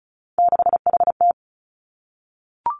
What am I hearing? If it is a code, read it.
Morse code, heard as a keyed tone, about 36 wpm, 700 Hz. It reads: 6HT
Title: Three Challenging Letter-Number Combinations 35wpm